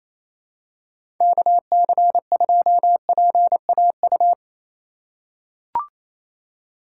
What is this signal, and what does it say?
Morse code, heard as a keyed tone, about 28 wpm, 700 Hz. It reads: KC2PAU
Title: USA Call Signs - 6 Characters - 2x Character Spacing - 28wpm